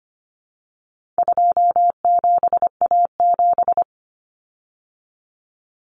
Morse code, heard as a keyed tone, about 25 wpm, 700 Hz. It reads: 27A7
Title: Common Cut Numbers - 4-Digit 25wpm